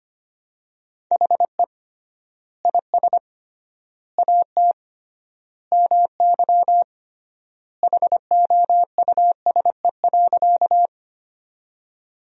Morse code, heard as a keyed tone, about 25 wpm, 700 Hz. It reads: HE IS AT MY HOUSE.
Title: 3x Word-Spacing - Sentences from Top 200 Words - 25wpm